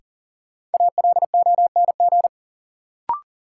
Morse code, heard as a keyed tone, about 40 wpm, 700 Hz. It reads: ALONG
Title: Sentences from Top 700 Words - New Word Drill - 40wpm